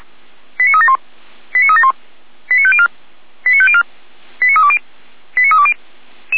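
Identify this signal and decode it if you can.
Five-tone selective call (ZVEI), heard as quick runs of five tones, tones of about 70 ms. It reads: 89371 89371 89504 89504 89320 89320
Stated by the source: ZVEI five-tone selective call received by SDR.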